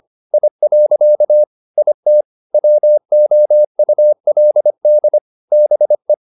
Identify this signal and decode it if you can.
Morse code, heard as a keyed tone, about 25 wpm, 600 Hz. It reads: I. IT WOULD BE